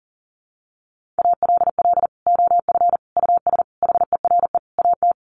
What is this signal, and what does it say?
Morse code, heard as a keyed tone, about 40 wpm, 700 Hz. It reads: ALL OF US HERE AT
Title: Sets of 5 Words 40wpm